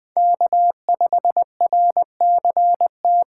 Morse code, heard as a keyed tone, about 21 wpm, 700 Hz. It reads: K5RCT